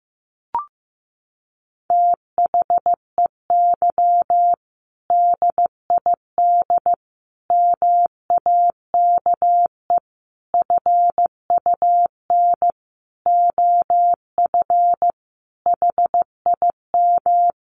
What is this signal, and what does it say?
Morse code, heard as a keyed tone, about 15 wpm, 700 Hz. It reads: THEY DID MAKE FUN OF HIM
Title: Sentences from Top 1300 Words 15wpm